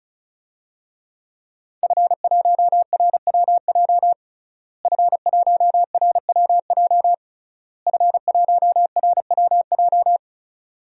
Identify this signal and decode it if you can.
Morse code, heard as a keyed tone, about 34 wpm, 700 Hz. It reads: F1RWJ F1RWJ F1RWJ
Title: DXCC-100 Call Signs - 5 Characters - Repeated 3x - 35wpm